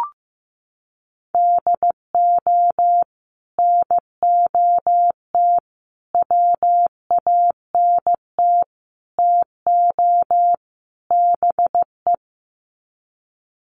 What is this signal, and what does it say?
Morse code, heard as a keyed tone, about 15 wpm, 700 Hz. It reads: DO NOT WANT TO BE
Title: Sets of 5 Words 15wpm